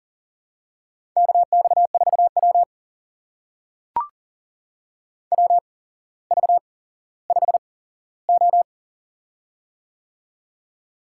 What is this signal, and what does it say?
Morse code, heard as a keyed tone, about 40 wpm, spaced as for 17 wpm, 700 Hz. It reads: KX4W WV5O
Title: USA Call Signs - 4 Characters - 8x Character Spacing - 40wpm